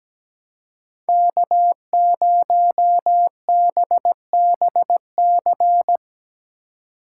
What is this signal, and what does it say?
Morse code, heard as a keyed tone, about 17 wpm, 700 Hz. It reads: K0BBC
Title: USA Call Signs - 5 Characters - 2x Character Spacing - 17wpm